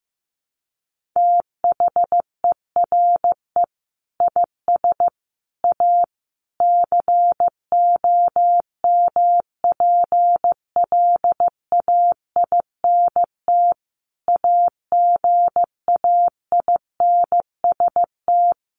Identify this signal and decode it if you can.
Morse code, heard as a keyed tone, about 15 wpm, 700 Hz. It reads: THERE IS A COMPLAINT AGAINST
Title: Sentences from Top 1400 Words 15wpm